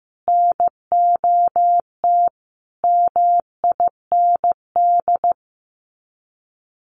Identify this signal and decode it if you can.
Morse code, heard as a keyed tone, about 15 wpm, 700 Hz. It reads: NOT MIND
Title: Sets of 5 Words 15wpm